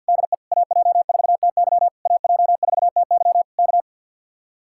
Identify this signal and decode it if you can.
Morse code, heard as a keyed tone, about 62 wpm, 700 Hz. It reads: DE AJ4TY AJ4TY K
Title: Speed-Racing Common QSO Phrases - 50wpm